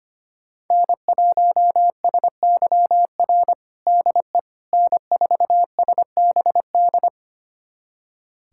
Morse code, heard as a keyed tone, about 25 wpm, 700 Hz. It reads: N1SYR DE N4SBD
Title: USA Callsign DE Callsign - 5 Characters - 25wpm